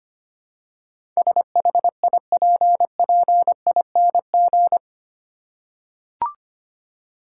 Morse code, heard as a keyed tone, about 25 wpm, 700 Hz. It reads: SHIPPING
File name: Eight Letter Words - 25wpm